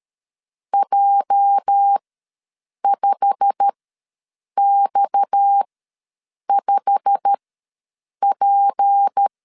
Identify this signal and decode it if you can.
Morse code, heard as a keyed tone, about 13 wpm, spaced as for 7 wpm, 800 Hz. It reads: J5X5P
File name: LEZIONI / 8 YJ45 / 8 YJ45 ws 12wpm 3146